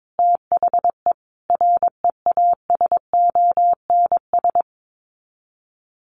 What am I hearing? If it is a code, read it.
Morse code, heard as a keyed tone, about 22 wpm, 700 Hz. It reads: THE REASONS
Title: Sets of 5 Words Encore2 22wpm